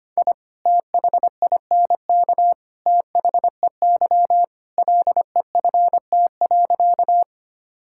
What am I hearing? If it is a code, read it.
Morse code, heard as a keyed tone, about 25 wpm, 700 Hz. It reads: I THINK THEY LEFT.
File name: practice / 25wpm / Sentences with 3-Word Phrases - 25wpm